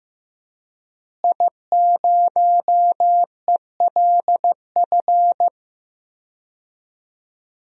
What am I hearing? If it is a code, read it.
Morse code, heard as a keyed tone, about 15 wpm, 700 Hz. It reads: I0ELF
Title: DXCC-100 Call Signs - 5 Characters - Repeated 2x - 15wpm